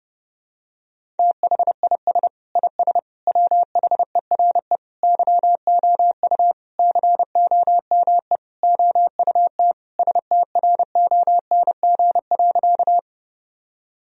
Morse code, heard as a keyed tone, about 30 wpm, 700 Hz. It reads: THIS IS WHERE YOU COME OUT STRONG.
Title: Sentences from Top 1000 Words - Review Part 3 - 30wpm